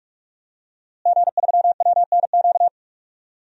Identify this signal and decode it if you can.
Morse code, heard as a keyed tone, about 44 wpm, 700 Hz. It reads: G3WNQ
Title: DXCC-100 Call Signs - 5 Characters - Repeated 3x - 45wpm